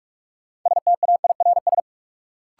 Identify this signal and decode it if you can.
Morse code, heard as a keyed tone, about 46 wpm, 700 Hz. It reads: STAIRS
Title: Sets of 5 Words Encore - New Word Drill - 45wpm